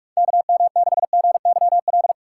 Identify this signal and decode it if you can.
Morse code, heard as a keyed tone, about 46 wpm, 700 Hz. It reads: KM6GYL
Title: USA Call Signs - 3x Character Spacing - 45wpm